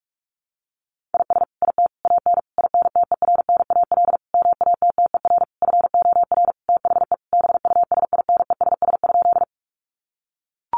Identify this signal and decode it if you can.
Morse code, heard as a keyed tone, about 44 wpm, 700 Hz. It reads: IS IT AN INTERNAL MATTER FOR THE BUSINESS?
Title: Sentences from Top 1400 Words 45wpm